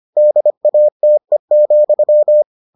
Morse code, heard as a keyed tone, about 25 wpm, 600 Hz. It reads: DATE,